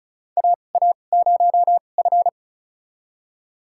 Morse code, heard as a keyed tone, about 36 wpm, spaced as for 25 wpm, 700 Hz. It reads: AA0F
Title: USA Call Signs - 4 Characters - 2x Character Spacing - 35wpm